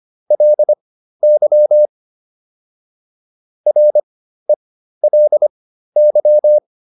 Morse code, heard as a keyed tone, about 25 wpm, spaced as for 13 wpm, 600 Hz. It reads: LY RELY